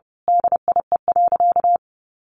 Morse code, heard as a keyed tone, about 30 wpm, 700 Hz. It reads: DIE.
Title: Sentences from Top 1300 Words 30wpm